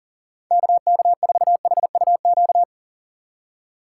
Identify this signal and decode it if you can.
Morse code, heard as a keyed tone, about 39 wpm, 700 Hz. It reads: KK4HUQ
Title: USA Call Signs - 6 Characters - 2x Character Spacing - 40wpm